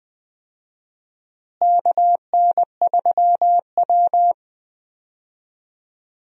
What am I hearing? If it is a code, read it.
Morse code, heard as a keyed tone, about 20 wpm, 700 Hz. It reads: KN3W
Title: USA Call Signs - 4 Characters - Repeated 2x - 20wpm